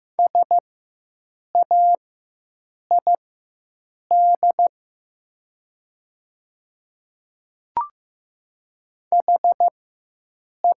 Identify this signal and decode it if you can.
Morse code, heard as a keyed tone, about 15 wpm, spaced as for 7 wpm, 700 Hz. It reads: SAID HE
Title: 4x Character Spacing - Top 500 Words - 15wpm